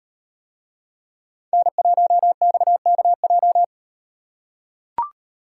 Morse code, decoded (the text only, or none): N1XKJ